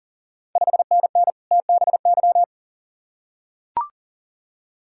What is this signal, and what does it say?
Morse code, heard as a keyed tone, about 39 wpm, 690 Hz. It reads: 5NN TBY
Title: Ontario QSO Party Exchange 40wpm